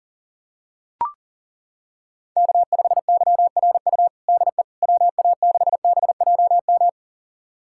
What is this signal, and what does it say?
Morse code, heard as a keyed tone, about 40 wpm, 700 Hz. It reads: K5YRU DE WA6BJM
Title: USA Callsign DE Callsign 40wpm